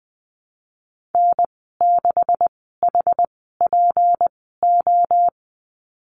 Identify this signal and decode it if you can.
Morse code, heard as a keyed tone, about 19 wpm, spaced as for 14 wpm, 700 Hz. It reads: N6HPO
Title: 179 Lesson 29g - Slash - Call Signs 20wpm